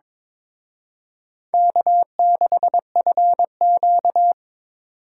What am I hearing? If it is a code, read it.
Morse code, heard as a keyed tone, about 26 wpm, 700 Hz. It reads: K6FQ
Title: Speed Racing - USA 1x2 Call Signs 22wpm